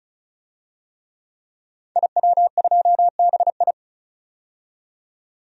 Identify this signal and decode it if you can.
Morse code, heard as a keyed tone, about 36 wpm, 700 Hz. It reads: IW2BI
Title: DXCC-100 Call Signs - 5 Characters - 35wpm